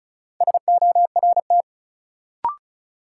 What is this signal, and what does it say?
Morse code, heard as a keyed tone, about 36 wpm, 700 Hz. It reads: SORT
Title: Sentences from Top 1300 Words - New Word Drill Alt - 35wpm